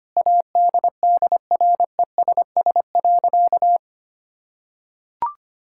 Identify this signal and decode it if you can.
Morse code, heard as a keyed tone, about 25 wpm, 700 Hz. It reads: ADDRESS.
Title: Sentences from Top 1000 Words - Review Part 6 - 25wpm